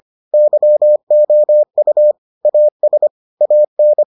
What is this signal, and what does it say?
Morse code, heard as a keyed tone, about 25 wpm, 600 Hz. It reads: YOU AS AN